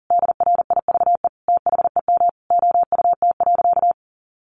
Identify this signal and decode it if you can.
Morse code, heard as a keyed tone, about 40 wpm, 700 Hz. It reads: DRIVE THEM OUT.
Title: Sentences from Top 500 Words 40wpm